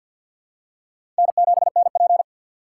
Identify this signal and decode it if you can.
Morse code, heard as a keyed tone, about 53 wpm, 700 Hz. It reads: N7NP
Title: USA Call Signs - 3x Character Spacing - 50wpm